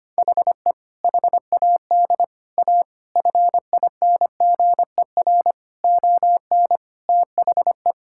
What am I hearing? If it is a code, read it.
Morse code, heard as a keyed tone, about 25 wpm, 700 Hz. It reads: HE HAD A FINGER ON THE
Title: Sentences from Top 1600 Words 25wpm